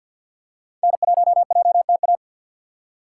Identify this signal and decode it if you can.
Morse code, heard as a keyed tone, about 48 wpm, 700 Hz. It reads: N1JTA
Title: USA Call Signs - 5 Characters - 6x Character Spacing - 50wpm